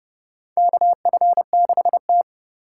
Morse code, heard as a keyed tone, about 30 wpm, 700 Hz. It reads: KF6T
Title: K1USN SST QSO Practice - 30wpm